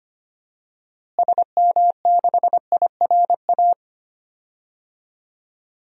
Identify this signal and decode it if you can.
Morse code, heard as a keyed tone, about 25 wpm, 700 Hz. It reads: SM6IRA
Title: DXCC-100 Call Signs - 6 Characters - Repeated 3x - 25wpm